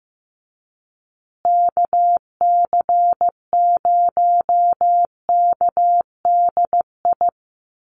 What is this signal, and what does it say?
Morse code, heard as a keyed tone, about 14 wpm, 700 Hz. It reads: KC0KDI
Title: USA Call Signs Repeated 3x 15wpm